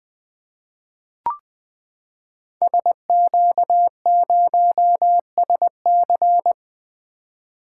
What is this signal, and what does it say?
Morse code, heard as a keyed tone, about 20 wpm, 700 Hz. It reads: SQ0SC
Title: Code Groups - 5 Letter:Numbers - 20wpm